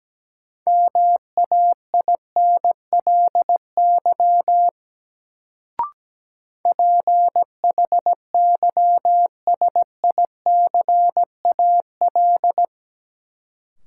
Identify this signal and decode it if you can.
Morse code, heard as a keyed tone, about 17 wpm, 700 Hz. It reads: MAINLY PHYSICAL
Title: Sentences from Top 1400 Words - New Word Drill Alt - 17wpm